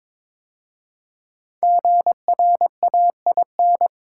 Morse code, heard as a keyed tone, about 22 wpm, 690 Hz. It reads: GRAIN
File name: 5 Letter Words - First 3 Letters Given - 22wpm